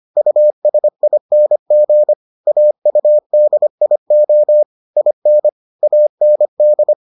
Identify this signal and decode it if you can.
Morse code, heard as a keyed tone, about 25 wpm, 600 Hz. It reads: USING AUDIO IN AND